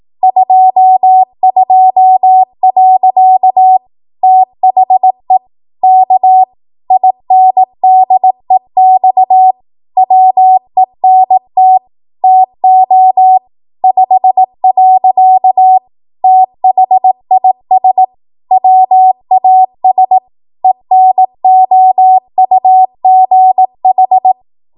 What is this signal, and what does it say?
Morse code, heard as a keyed tone, about 18 wpm, 750 Hz. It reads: 22. THE K INDEX WENT TO 5. THIS WAS ENOUGH
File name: Archive / 18 WPM / 18WPM